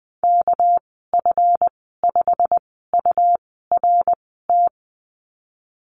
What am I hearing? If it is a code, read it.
Morse code, heard as a keyed tone, about 20 wpm, spaced as for 15 wpm, 700 Hz. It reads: KF5URT
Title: 213 Lesson 34g - Number 8 - Call Signs 20wpm